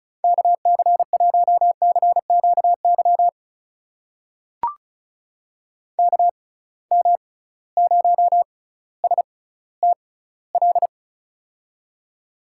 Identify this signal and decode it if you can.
Morse code, heard as a keyed tone, about 34 wpm, spaced as for 18 wpm, 700 Hz. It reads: KC1CQY KM0STL
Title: USA Call Signs - 6 Characters - 6x Character Spacing - 35wpm